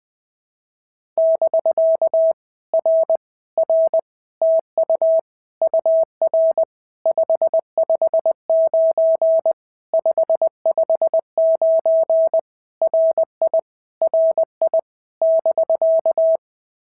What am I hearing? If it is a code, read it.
Morse code, heard as a keyed tone, about 20 wpm, 650 Hz. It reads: <BK> R R TU UR 559 559 RI RI <BK>
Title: POTA QSO Practice - 20wpm